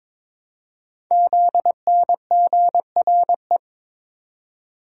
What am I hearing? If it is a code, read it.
Morse code, heard as a keyed tone, about 23 wpm, 700 Hz. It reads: ZNGRE
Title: Code Groups - 5 Letters - 22wpm